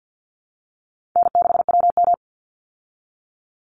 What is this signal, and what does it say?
Morse code, heard as a keyed tone, about 50 wpm, 700 Hz. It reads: N6WM
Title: Easy Call Signs Repeated 2x - 50wpm